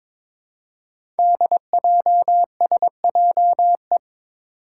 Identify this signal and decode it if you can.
Morse code, heard as a keyed tone, about 22 wpm, 700 Hz. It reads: DJSJE
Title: Five Challenging Letter-Number Combinations 22wpm